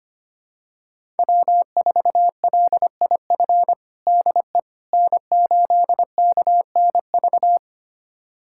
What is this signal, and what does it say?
Morse code, heard as a keyed tone, about 25 wpm, 700 Hz. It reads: W4LIF DE N8KNV
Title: USA Callsign DE Callsign - 5 Characters - 25wpm